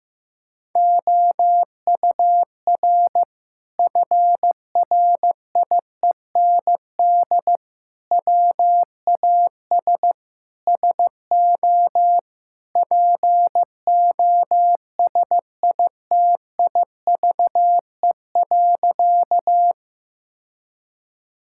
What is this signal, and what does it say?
Morse code, heard as a keyed tone, about 15 wpm, 700 Hz. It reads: OUR FRIEND WAS SO POSITIVE.
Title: Sentences from Top 1000 Words - Review Part 5 - 15wpm